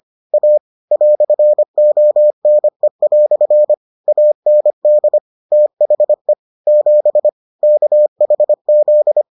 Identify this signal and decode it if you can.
Morse code, heard as a keyed tone, about 25 wpm, 600 Hz. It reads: A "ONE" AND THE 7 KHZ